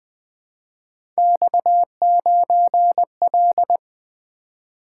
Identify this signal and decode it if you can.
Morse code, heard as a keyed tone, about 20 wpm, 700 Hz. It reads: X9L
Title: Three Letter-Number Combinations - 20wpm